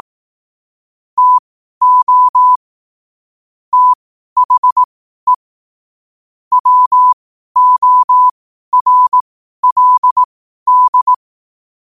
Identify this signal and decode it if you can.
Morse code, heard as a keyed tone, about 18 wpm, spaced as for 12 wpm, 1000 Hz. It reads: TO THE WORLD
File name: qotdfiles / 12wpm